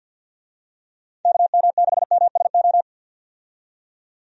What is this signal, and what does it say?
Morse code, heard as a keyed tone, about 45 wpm, 700 Hz. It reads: KM6MSO
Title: USA Call Signs - 6 Characters - 50wpm